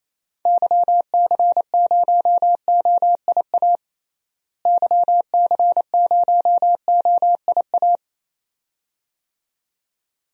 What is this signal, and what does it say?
Morse code, heard as a keyed tone, about 28 wpm, 700 Hz. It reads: YC0OIA YC0OIA
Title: DXCC-100 Call Signs - 6 Characters - Repeated 2x - 28wpm